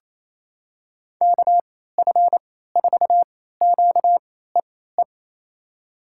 Morse code, heard as a keyed tone, about 28 wpm, spaced as for 16 wpm, 700 Hz. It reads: KF4QEE
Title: USA Call Signs - 6 Characters - 3x Character Spacing - 28wpm